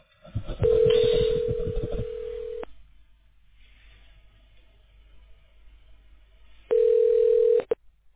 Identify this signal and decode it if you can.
Ringback tone (North America), 440 + 480 Hz.